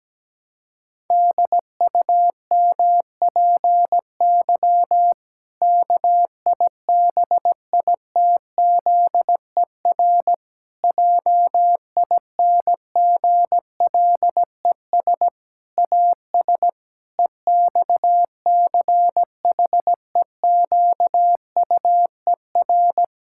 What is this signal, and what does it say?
Morse code, heard as a keyed tone, about 17 wpm, 700 Hz. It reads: DUMPY KIBITZER JINGLES AS EXCHEQUER
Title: Pangram Challenge v2 - 17wpm